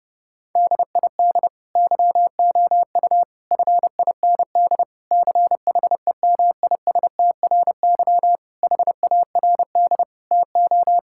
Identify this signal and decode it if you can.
Morse code, heard as a keyed tone, about 30 wpm, 700 Hz. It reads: DID YOU FIND CHEMISTRY HARD TO
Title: Sentences from Top 1000 Words - Review Part 6 - 30wpm